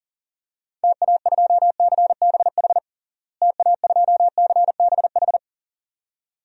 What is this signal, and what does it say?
Morse code, heard as a keyed tone, about 40 wpm, 700 Hz. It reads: TA2CBH TA2CBH
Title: DXCC-100 Call Signs - 6 Characters - Repeated 2x - 40wpm